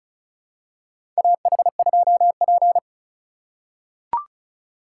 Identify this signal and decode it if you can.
Morse code, heard as a keyed tone, about 34 wpm, 700 Hz. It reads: AH2P
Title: USA Call Signs - 4 Characters - 8x Character Spacing - 35wpm